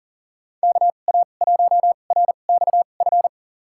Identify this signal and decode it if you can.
Morse code, heard as a keyed tone, about 40 wpm, spaced as for 29 wpm, 700 Hz. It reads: KA1RXF
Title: USA Call Signs - 6 Characters - 2x Character Spacing - 40wpm